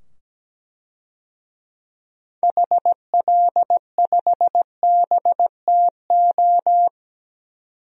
Morse code, heard as a keyed tone, about 17 wpm, 700 Hz. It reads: HL5BTO
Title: DXCC-100 Call Signs - 6 Characters - 17wpm